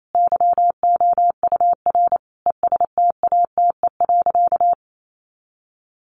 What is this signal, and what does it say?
Morse code, heard as a keyed tone, about 28 wpm, 700 Hz. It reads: YOUR ESTATE.
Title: Sentences from Top 1200 Words 28wpm